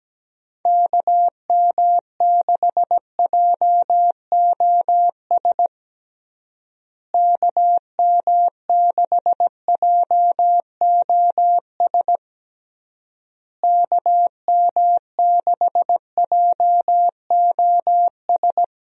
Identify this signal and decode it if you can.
Morse code, heard as a keyed tone, about 17 wpm, 700 Hz. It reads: KM6JOS KM6JOS KM6JOS
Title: USA Call Signs - 6 Characters - Repeated 3x - 17wpm